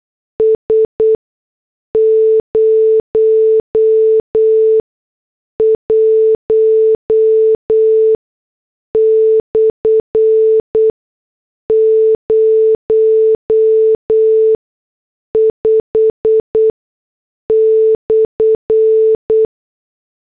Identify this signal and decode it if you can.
Morse code, heard as a keyed tone, about 8 wpm, spaced as for 6 wpm, 440 Hz. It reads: S 01/05/
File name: morse / headlines